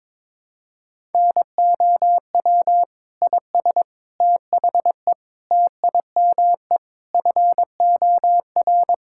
Morse code, heard as a keyed tone, about 22 wpm, 700 Hz. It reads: NOW IS THE TIME FOR
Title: Pangram Challenge v2 - 22wpm